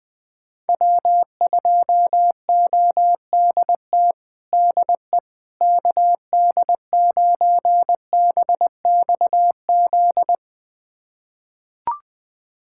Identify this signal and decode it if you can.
Morse code, heard as a keyed tone, about 20 wpm, 700 Hz. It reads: W2ODT DE KD9BXZ
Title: USA Callsign DE Callsign 20wpm